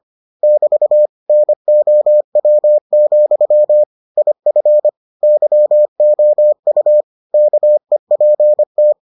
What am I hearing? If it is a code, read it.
Morse code, heard as a keyed tone, about 25 wpm, 600 Hz. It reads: <BT> NOW, IF YOU KEPT